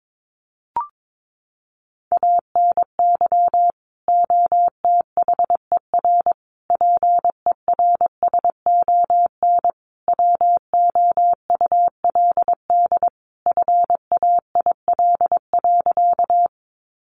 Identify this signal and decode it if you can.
Morse code, heard as a keyed tone, about 22 wpm, 700 Hz. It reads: ANY OTHER PERSON WOULD FAIL.
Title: Sentences from Top 1100 Words 22wpm